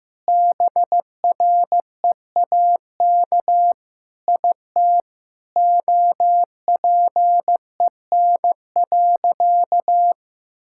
Morse code, heard as a keyed tone, about 15 wpm, 700 Hz. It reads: BREAK IT OPEN.
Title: Sentences from Top 1000 Words - Review Part 6 - 15wpm